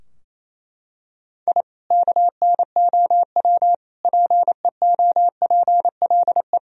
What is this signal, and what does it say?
Morse code, heard as a keyed tone, about 28 wpm, 700 Hz. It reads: I KNOW PEOPLE
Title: Sentences from Top 1000 Words - Review Part 4 - 28wpm